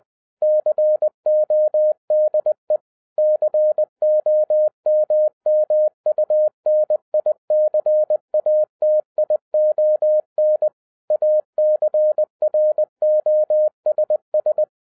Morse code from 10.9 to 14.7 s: ACROSS